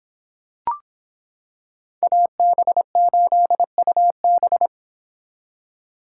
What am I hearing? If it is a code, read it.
Morse code, heard as a keyed tone, about 26 wpm, 700 Hz. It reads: AB8UB